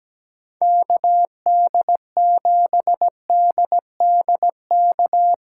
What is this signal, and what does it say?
Morse code, heard as a keyed tone, about 18 wpm, 700 Hz. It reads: KD7DDK
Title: Speed Racing - USA 2x3 Call Signs 17wpm